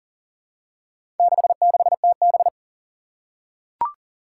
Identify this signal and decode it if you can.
Morse code, heard as a keyed tone, about 40 wpm, 690 Hz. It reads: 66TB
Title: Four Challenging Letter-Number Combinations 40wpm